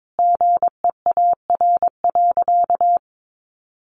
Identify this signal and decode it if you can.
Morse code, heard as a keyed tone, about 22 wpm, 700 Hz. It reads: GEAR.
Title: Sentences from Top 1400 Words 22wpm